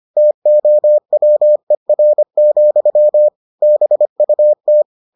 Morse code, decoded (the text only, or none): TOWER, BUT